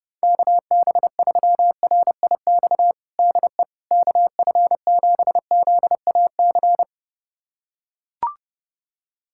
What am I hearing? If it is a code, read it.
Morse code, heard as a keyed tone, about 30 wpm, 700 Hz. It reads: KB3RIX DE KF7ZAC
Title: USA Callsign DE Callsign - 6 Characters - 30wpm